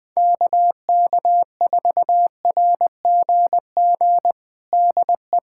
Morse code, heard as a keyed tone, about 20 wpm, 700 Hz. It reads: KK4RGG DE
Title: USA Callsign DE Callsign 20wpm